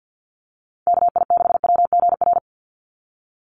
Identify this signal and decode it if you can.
Morse code, heard as a keyed tone, about 50 wpm, 700 Hz. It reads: KI6WGR
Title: USA Call Signs 50wpm